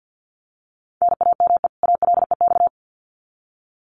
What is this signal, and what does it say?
Morse code, heard as a keyed tone, about 50 wpm, 700 Hz. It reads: NAME ALEX
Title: Common QSO Phrases 50wpm